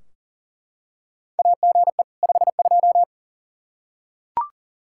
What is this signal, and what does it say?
Morse code, heard as a keyed tone, about 37 wpm, 700 Hz. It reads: AGE 52
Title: Practice QSO - Exchange 3 - More Info - 40wpm